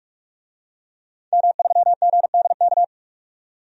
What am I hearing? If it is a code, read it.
Morse code, heard as a keyed tone, about 46 wpm, 700 Hz. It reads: M3GDK
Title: DXCC-100 Call Signs - 5 Characters - 45wpm